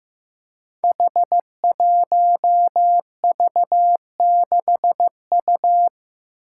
Morse code, heard as a keyed tone, about 15 wpm, 700 Hz. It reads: H1V6U
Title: Five Challenging Letter-Number Combinations 15wpm